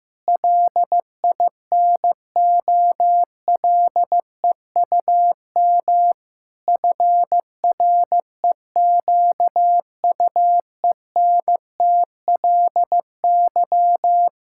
Morse code, heard as a keyed tone, about 15 wpm, 700 Hz. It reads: LINOLEUM FREQUENTLY
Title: Pangram Challenge v2 - 15wpm